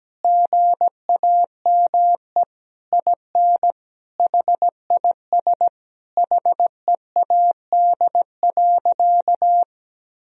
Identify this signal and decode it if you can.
Morse code, heard as a keyed tone, about 17 wpm, 700 Hz. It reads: GAME IN HIS HEAD.
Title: Sentences from Top 1000 Words - Review Part 3 - 17wpm